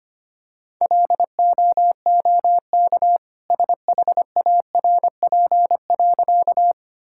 Morse code, heard as a keyed tone, about 25 wpm, 700 Hz. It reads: LOOK SHARP.
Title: Sentences from Top 1600 Words 25wpm